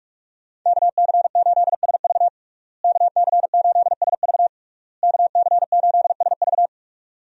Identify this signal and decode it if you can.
Morse code, heard as a keyed tone, about 46 wpm, 700 Hz. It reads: KC8SV KC8SV KC8SV
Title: USA Call Signs - 5 Characters - Repeated 3x - 45wpm